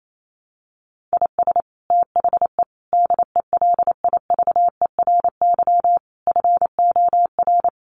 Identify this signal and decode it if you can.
Morse code, heard as a keyed tone, about 28 wpm, 700 Hz. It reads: IS THE DELIVERY FOR
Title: Sentences from Top 1500 Words 28wpm